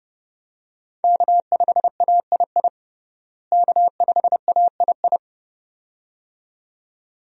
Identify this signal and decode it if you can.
Morse code, heard as a keyed tone, about 30 wpm, 700 Hz. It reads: K5AII K5AII
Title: USA Call Signs - 5 Characters - Repeated 2x - 30wpm